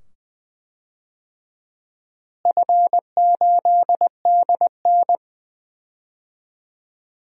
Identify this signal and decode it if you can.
Morse code, heard as a keyed tone, about 20 wpm, 700 Hz. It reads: F8DN
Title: DXCC-100 Call Signs - 4 Characters - Repeated 2x - 20wpm